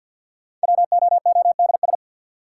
Morse code, heard as a keyed tone, about 53 wpm, 700 Hz. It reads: WOODS
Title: Sentences from Top 1000 Words - Review Part 2 - New Word Drill - 50wpm